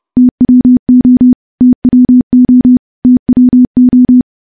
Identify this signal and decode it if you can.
Morse code, heard as a keyed tone, about 32 wpm, 250 Hz. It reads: TWO TWO TWO
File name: morse / morse-training-30WPM